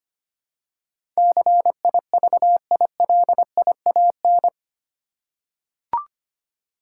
Morse code, heard as a keyed tone, about 25 wpm, 700 Hz. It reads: CIVILIAN
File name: Eight Letter Words - 25wpm